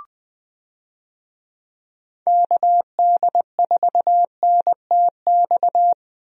Morse code, heard as a keyed tone, about 21 wpm, 700 Hz. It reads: KD4NTX
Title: USA Call Signs - 6 Characters - Repeated 2x - 20wpm